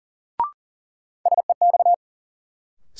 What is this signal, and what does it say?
Morse code, heard as a keyed tone, about 40 wpm, 700 Hz. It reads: SEX
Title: Sentences from Top 1600 Words - New Word Drill Alt - 40wpm